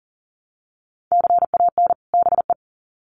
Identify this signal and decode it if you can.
Morse code, heard as a keyed tone, about 40 wpm, 700 Hz. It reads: CAN BE
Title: Sets of 2 Words 40wpm